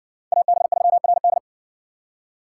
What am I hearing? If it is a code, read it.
Morse code, heard as a keyed tone, about 66 wpm, 700 Hz. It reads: AB3RD